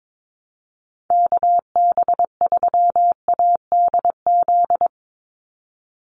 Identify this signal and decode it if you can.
Morse code, heard as a keyed tone, about 22 wpm, 700 Hz. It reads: KB3ADZ
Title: USA Call Signs 22wpm